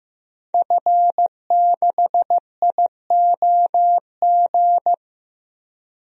Morse code, heard as a keyed tone, about 15 wpm, 700 Hz. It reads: F6IOG
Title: DXCC-100 Call Signs - 5 Characters - Repeated 2x - 15wpm